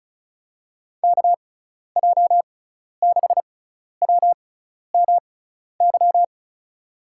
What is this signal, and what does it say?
Morse code, heard as a keyed tone, about 34 wpm, spaced as for 12 wpm, 700 Hz. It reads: KJ6WMY
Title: USA Call Signs - 6 Characters - 6x Character Spacing - 35wpm